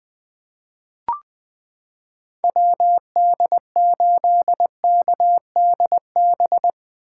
Morse code, heard as a keyed tone, about 20 wpm, 700 Hz. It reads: WD8KDB
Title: 2023 SOTA Chaser Call Signs - 20wpm